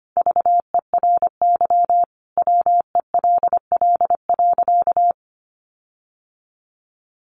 Morse code, heard as a keyed tone, about 25 wpm, 700 Hz. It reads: VERY WELL.
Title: Sentences from Top 1300 Words 25wpm